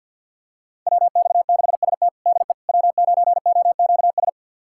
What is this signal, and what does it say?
Morse code, heard as a keyed tone, about 48 wpm, 700 Hz. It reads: WX6ST DE W9OQS
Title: USA Callsign DE Callsign - 5 Characters - 50wpm